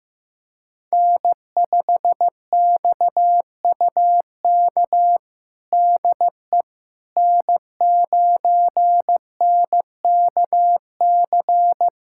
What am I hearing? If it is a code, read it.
Morse code, heard as a keyed tone, about 15 wpm, 700 Hz. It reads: N5XUK DE N9NKC
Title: USA Callsign DE Callsign - 5 Characters - 15wpm